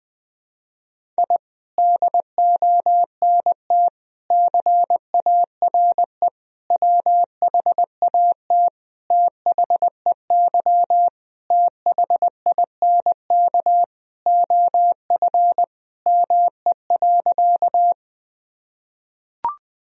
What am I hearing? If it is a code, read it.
Morse code, heard as a keyed tone, about 20 wpm, 700 Hz. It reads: I DONT CARE WHAT THEY THINK OF ME.
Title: Sentences from Top 1000 Words - Review Part 1 - 20wpm